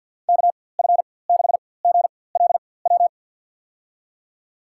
Morse code, heard as a keyed tone, about 48 wpm, spaced as for 23 wpm, 700 Hz. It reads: KF6GLW
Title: USA Call Signs - 4x Character Spacing - 50wpm